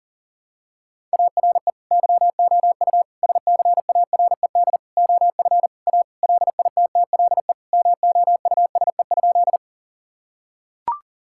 Morse code, heard as a keyed tone, about 40 wpm, 700 Hz. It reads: ARE YOU SCARED OF A LITTLE MOUSE?
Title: Sentences from Top 1000 Words - Review Part 5 - 40wpm